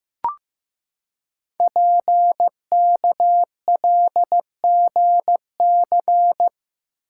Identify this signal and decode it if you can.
Morse code, heard as a keyed tone, about 15 wpm, 700 Hz. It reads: PKLGC
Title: Code Groups - 5 Letter:Numbers - 15wpm